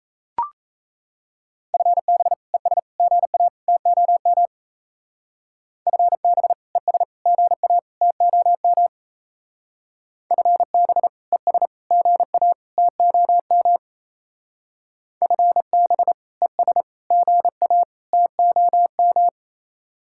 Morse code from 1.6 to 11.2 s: FB ES GA TOM FB ES GA TOM FB

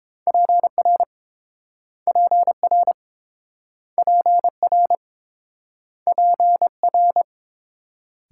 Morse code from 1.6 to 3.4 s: PR

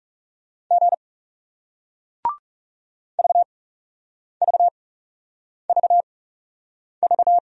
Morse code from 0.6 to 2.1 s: G

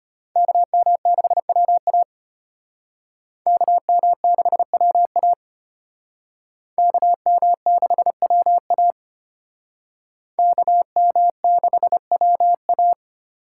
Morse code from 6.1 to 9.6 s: KM6WA

